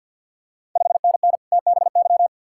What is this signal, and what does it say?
Morse code, heard as a keyed tone, about 53 wpm, 690 Hz. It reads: 5NN TBY
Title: Ontario QSO Party Exchange 50wpm